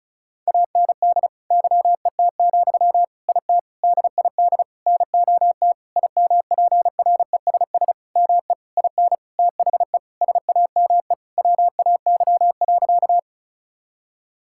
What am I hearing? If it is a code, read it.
Morse code, heard as a keyed tone, about 34 wpm, 700 Hz. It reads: AND YET, IT DID NOT IMPRESS ME IN THE SAME WAY.
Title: Sentences from Top 1600 Words 35wpm